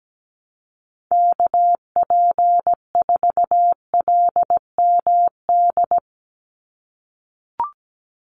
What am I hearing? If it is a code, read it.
Morse code, heard as a keyed tone, about 17 wpm, 700 Hz. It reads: KP4LMD
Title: USA Call Signs 17wpm